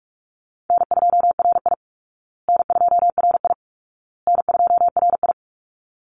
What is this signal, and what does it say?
Morse code, heard as a keyed tone, about 44 wpm, 700 Hz. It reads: N2RI N2RI N2RI
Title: Easy Call Signs Repeated 3x - 45wpm